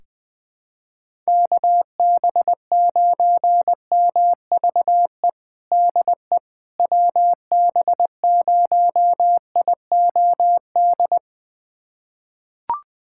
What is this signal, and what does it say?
Morse code, heard as a keyed tone, about 20 wpm, 700 Hz. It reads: KB9MVE DE WB0IOD
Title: USA Callsign DE Callsign 20wpm